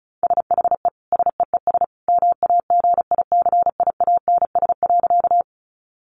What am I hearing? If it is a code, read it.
Morse code, heard as a keyed tone, about 34 wpm, 700 Hz. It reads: SHE SEES MAGICIANS.
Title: Easy Phrases 35wpm